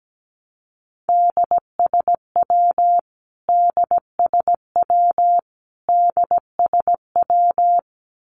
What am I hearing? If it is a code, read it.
Morse code, heard as a keyed tone, about 17 wpm, 700 Hz. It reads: DSW DSW DSW
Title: Abbreviations Repeated 3x - 17wpm